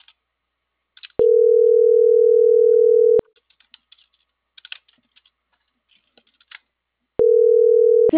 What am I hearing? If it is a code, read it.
Ringback tone (North America), 440 + 480 Hz.